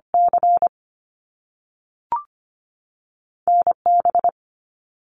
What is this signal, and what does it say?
Morse code, heard as a keyed tone, about 25 wpm, 700 Hz. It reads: C NB
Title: Canadian Provinces and Territories 25wpm